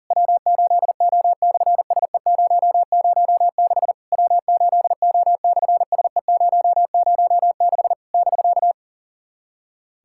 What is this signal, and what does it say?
Morse code, heard as a keyed tone, about 40 wpm, 700 Hz. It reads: W8O/SE006 W8O/SE006 <BK>
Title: SOTA QSO Practice 40wpm